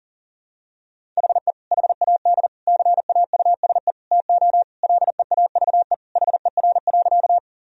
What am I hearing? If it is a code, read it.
Morse code, heard as a keyed tone, about 40 wpm, 700 Hz. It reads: HE HAD CAUSE TO LEAVE HER.
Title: Sentences from Top 1000 Words - Review Part 1 - 40wpm